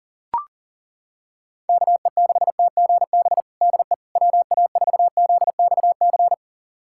Morse code, heard as a keyed tone, about 40 wpm, 700 Hz. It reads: KE6TGB DE WA4ZXC
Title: USA Callsign DE Callsign 40wpm